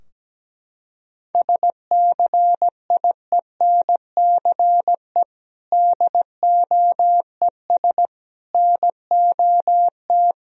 Morse code, heard as a keyed tone, about 17 wpm, 700 Hz. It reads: SCIENCE DOES NOT
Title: Sentences from Top 1000 Words - Review Part 2 - 17wpm